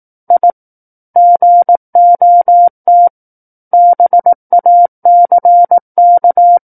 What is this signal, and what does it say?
Morse code, heard as a keyed tone, about 18 wpm, 700 Hz. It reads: I GOT BACK
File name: 18 WPM